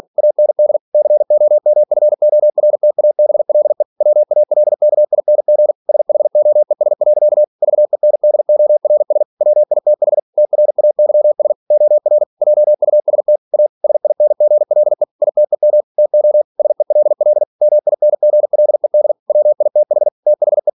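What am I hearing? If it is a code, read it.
Morse code, heard as a keyed tone, about 46 wpm, 600 Hz. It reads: AND COMFORTABLE WALKING SHOES. VENDORS WITH TRAYS OR JUST A SINGLE ITEM TO SELL MINGLED WITH THE
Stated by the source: Practice text from an English novel sent at 47 WPM.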